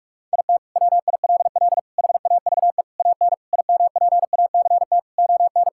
Morse code, heard as a keyed tone, about 41 wpm, 700 Hz. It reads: IT WILL HAVE AN IMPACT ON
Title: Sentences from Top 1000 Words - Review Part 6 - 45wpm